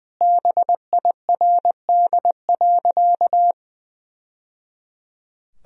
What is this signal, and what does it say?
Morse code, heard as a keyed tone, about 21 wpm, 700 Hz. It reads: BIRD.
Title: Sentences from Top 1000 Words - Review Part 2 - 20wpm